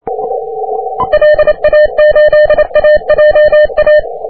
Morse code, heard as a keyed tone, about 29 wpm, 1800 Hz. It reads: LA8AJA
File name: wp / s04r 30m cw